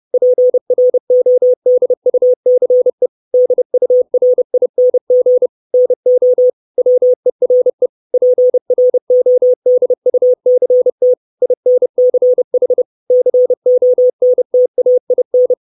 Morse code, heard as a keyed tone, about 30 wpm, 500 Hz. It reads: PRODUCE DURING NO WERE PRODUCT INCH CONTAIN
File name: 632 Words 30WPM